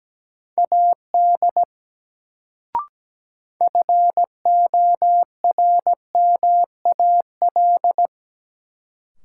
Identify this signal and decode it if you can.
Morse code, heard as a keyed tone, about 17 wpm, 700 Hz. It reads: AD FORMAL
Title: Sentences from Top 1300 Words - New Word Drill - 17wpm